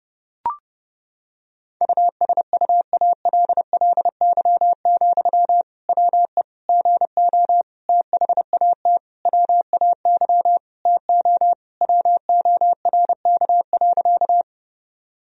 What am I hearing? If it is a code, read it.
Morse code, heard as a keyed tone, about 30 wpm, 700 Hz. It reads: USUALLY, WE GO THAT WAY TO WORK.
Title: Sentences from Top 1000 Words - Review Part 3 - 30wpm